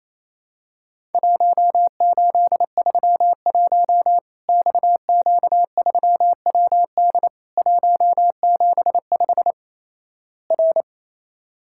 Morse code, heard as a keyed tone, about 28 wpm, 700 Hz. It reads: 1831 XQ3WD 175 R
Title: Worked All Europe DX Contest - QTC Exchange - 28wpm